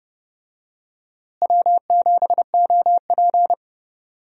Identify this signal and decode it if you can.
Morse code, heard as a keyed tone, about 31 wpm, 700 Hz. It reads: W7OP
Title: USA Call Signs - 4 Characters - Repeated 2x - 30wpm